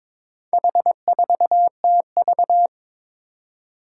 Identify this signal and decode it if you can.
Morse code, heard as a keyed tone, about 22 wpm, 700 Hz. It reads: H4TV
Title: Four Challenging Letter-Number Combinations 22wpm